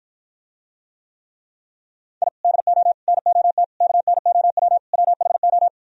Morse code, heard as a keyed tone, about 52 wpm, 700 Hz. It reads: I DO NOT KNOW WHO